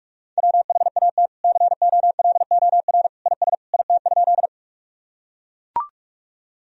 Morse code, heard as a keyed tone, about 42 wpm, 700 Hz. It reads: WHAT COLOR IS IT?
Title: Sentences from Top 1000 Words - Review Part 2 - 45wpm